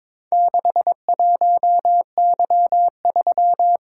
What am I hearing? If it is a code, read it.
Morse code, heard as a keyed tone, about 22 wpm, 700 Hz. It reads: 61Y3
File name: practice / 22wpm / Four Letter-Number Combinations - 22wpm